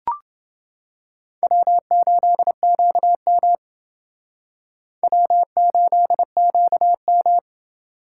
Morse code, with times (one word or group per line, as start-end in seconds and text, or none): 1.4-3.5 W8QM
5.0-7.4 W8QM